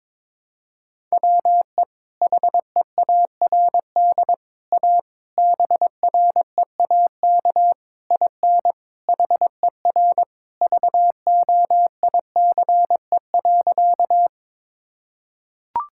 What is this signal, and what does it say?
Morse code, heard as a keyed tone, about 22 wpm, 700 Hz. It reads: WE HEARD A BREAK IN HER VOICE.
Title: Sentences from Top 1000 Words - Review Part 6 - 22wpm